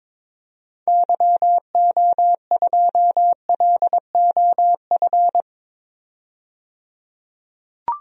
Morse code, heard as a keyed tone, about 22 wpm, 700 Hz. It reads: YO2LOF
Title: DXCC-100 Call Signs - 6 Characters - 22wpm